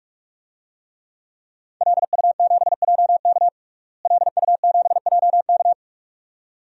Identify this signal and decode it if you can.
Morse code, heard as a keyed tone, about 45 wpm, 700 Hz. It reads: LU7JK LU7JK
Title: DXCC-100 Call Signs - 5 Characters - Repeated 2x - 45wpm